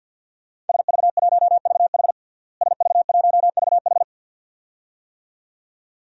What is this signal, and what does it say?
Morse code, heard as a keyed tone, about 47 wpm, 700 Hz. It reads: SV1VH SV1VH
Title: DXCC-100 Call Signs - 5 Characters - Repeated 2x - 50wpm